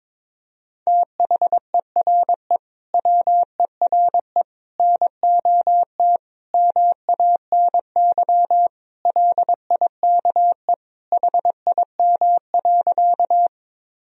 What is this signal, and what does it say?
Morse code, heard as a keyed tone, about 22 wpm, 700 Hz. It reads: THERE WERE NOT MANY LIKE HIM.
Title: Sentences from Top 1000 Words - Review Part 1 - 22wpm